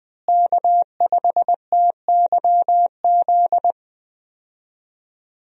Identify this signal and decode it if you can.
Morse code, heard as a keyed tone, about 21 wpm, 700 Hz. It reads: K5TYZ
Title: USA Call Signs - 5 Characters - Repeated 3x - 20wpm